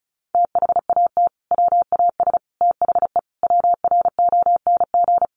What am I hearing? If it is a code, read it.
Morse code, heard as a keyed tone, about 34 wpm, 700 Hz. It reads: THAT WAS THE WRONG
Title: Sentences from Top 900 Words 35wpm